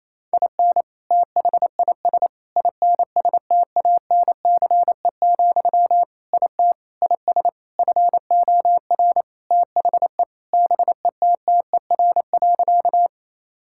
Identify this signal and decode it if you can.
Morse code, heard as a keyed tone, about 28 wpm, 700 Hz. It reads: IN THIS INSTANCE, IT IS FOR THE BETTER.
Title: Sentences from Top 1000 Words - Review Part 5 - 28wpm